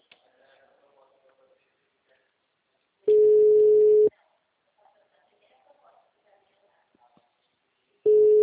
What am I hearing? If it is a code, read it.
Ringback tone (Europe, 425 Hz), with speech.
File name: out- -2034- - - .25317